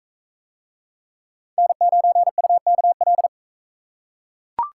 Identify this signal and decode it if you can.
Morse code, heard as a keyed tone, about 39 wpm, 700 Hz. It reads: N9UKL